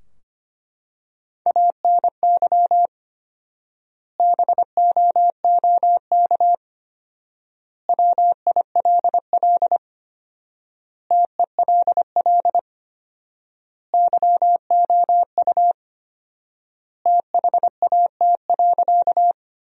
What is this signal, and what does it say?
Morse code, heard as a keyed tone, about 25 wpm, 700 Hz. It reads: ANY BOOK WILL TELL YOU THAT.
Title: 4x Word-Spacing - Sentences from Top 300 Words - 25wpm